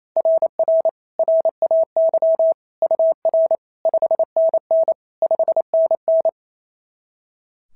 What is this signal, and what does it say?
Morse code, heard as a keyed tone, about 27 wpm, 650 Hz. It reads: RR RAY UR 5NN 5NN
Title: SOTA QSO Practice 28wpm